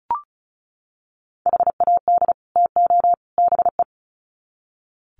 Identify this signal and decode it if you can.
Morse code, heard as a keyed tone, about 34 wpm, 700 Hz. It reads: HAD TO BE
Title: Sets of 3 Words 35wpm